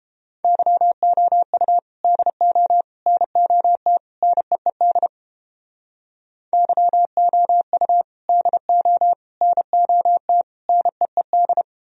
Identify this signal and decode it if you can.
Morse code, heard as a keyed tone, about 33 wpm, 700 Hz. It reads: YOU DO NOT NEED YOU DO NOT NEED